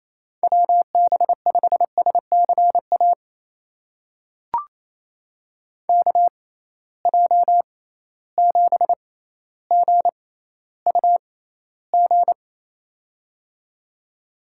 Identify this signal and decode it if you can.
Morse code, heard as a keyed tone, about 28 wpm, spaced as for 14 wpm, 700 Hz. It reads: WB5SCA KJ7GUG
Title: USA Call Signs - 6 Characters - 6x Character Spacing - 28wpm